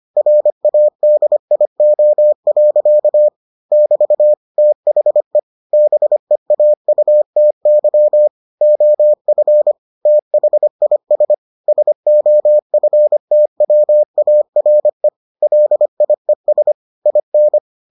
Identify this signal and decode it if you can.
Morse code, heard as a keyed tone, about 25 wpm, 600 Hz. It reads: RADIO. <BT> THE BEAUTY OF THIS SOFTWARE LIES IN